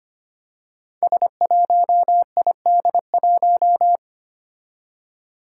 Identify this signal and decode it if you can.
Morse code, heard as a keyed tone, about 25 wpm, 700 Hz. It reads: S1ID1
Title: Five Challenging Letter-Number Combinations 25wpm